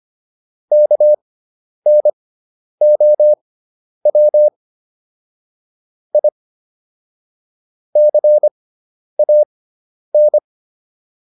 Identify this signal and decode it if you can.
Morse code, heard as a keyed tone, about 25 wpm, spaced as for 10 wpm, 600 Hz. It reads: KNOW I CAN